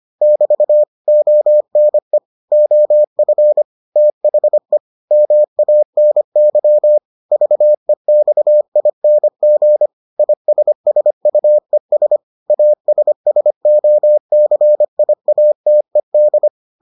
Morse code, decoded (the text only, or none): <BT> ONE OF THE MANY VEXING ISSUES ASSOCIATED